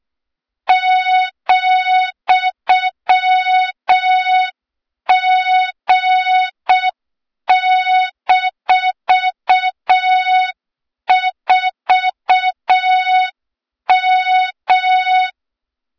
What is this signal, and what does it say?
Morse code, heard as a keyed tone, about 6 wpm, 3700 Hz, with noise. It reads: ,G-4M
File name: Morze-HA2MN / hang / 23 GYAKORLAT